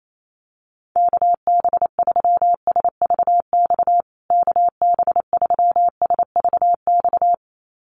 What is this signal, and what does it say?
Morse code, heard as a keyed tone, about 28 wpm, 700 Hz. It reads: KB3SVX KB3SVX
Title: USA Call Signs Repeated 2x 28wpm